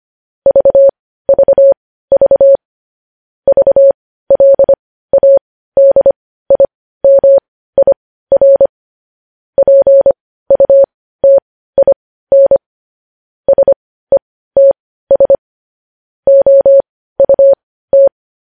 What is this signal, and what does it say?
Morse code, heard as a keyed tone, about 25 wpm, spaced as for 15 wpm, 550 Hz. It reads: VVV VLADIMIR PUTIN SETS OUT
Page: news headlines in morse code at 15 wpm